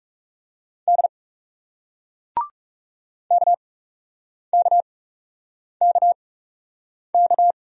Morse code, heard as a keyed tone, about 40 wpm, spaced as for 8 wpm, 700 Hz. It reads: D KKKK